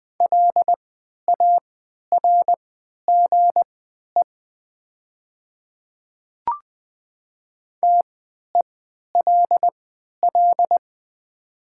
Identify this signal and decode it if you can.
Morse code, heard as a keyed tone, about 20 wpm, spaced as for 11 wpm, 700 Hz. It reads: LARGE TELL
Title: 3x Character Spacing - Top 300 Words - 20wpm